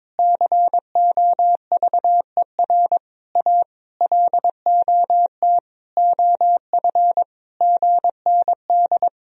Morse code, heard as a keyed tone, about 22 wpm, 700 Hz. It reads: COVER A LOT OF GND
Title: Sentences from Top 500 Words and Abbreviations 22wpm